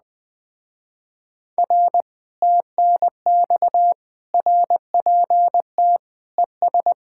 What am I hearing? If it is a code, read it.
Morse code, heard as a keyed tone, about 20 wpm, 700 Hz. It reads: R TNX RPT ES
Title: Speed-Racing Common QSO Phrases - 17wpm